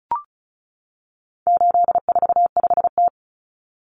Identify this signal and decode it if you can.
Morse code, heard as a keyed tone, about 34 wpm, 700 Hz. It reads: 845T
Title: Common Cut Numbers - 4-Digit 35wpm